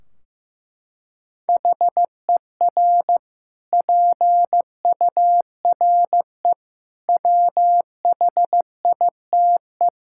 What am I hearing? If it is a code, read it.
Morse code, heard as a keyed tone, about 15 wpm, 700 Hz. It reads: HER PURE WHITE
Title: Sentences from Top 1600 Words 15wpm